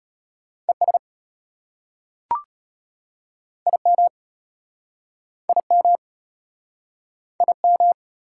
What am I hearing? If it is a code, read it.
Morse code, heard as a keyed tone, about 33 wpm, 700 Hz. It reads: ES IM IM IM